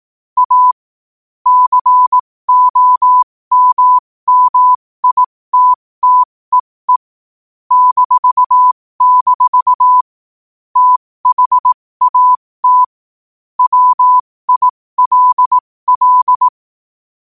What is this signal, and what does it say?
Morse code, heard as a keyed tone, about 18 wpm, spaced as for 15 wpm, 1000 Hz. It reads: A COMMITTEE -- THAT WILL
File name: qotdfiles / 15wpm